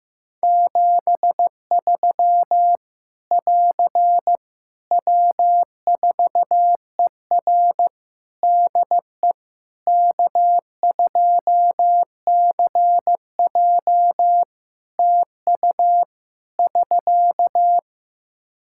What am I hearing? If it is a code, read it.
Morse code, heard as a keyed tone, about 16 wpm, 700 Hz. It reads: 73 <AR> W4ER DE K2CJ TU <SK>
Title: Practice QSO - Exchange 4 - Ending - 15wpm